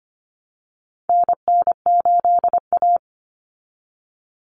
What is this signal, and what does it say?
Morse code, heard as a keyed tone, about 25 wpm, 700 Hz. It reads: NN8A
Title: Common Cut Numbers - 4-Digit 25wpm